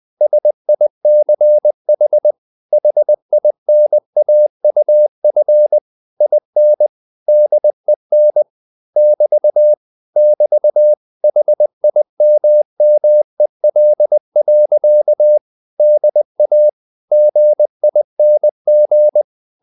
Morse code, heard as a keyed tone, about 20 wpm, 600 Hz. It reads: SICH HINAUF IN DEN <BT> <BT> HIMMEL. DA GING